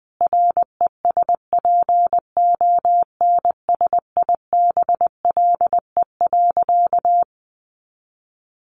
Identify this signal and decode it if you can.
Morse code, heard as a keyed tone, about 20 wpm, 700 Hz. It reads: RESPONSIBLE.
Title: A Sound Of Thunder 20wpm